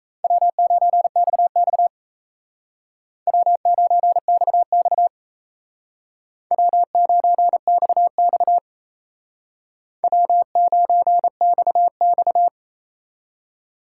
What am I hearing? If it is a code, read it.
Morse code, heard as a keyed tone, about 35 wpm, 700 Hz. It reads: W9XX W9XX W9XX W9XX
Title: Speed Racing - USA 1x2 Call Signs 28wpm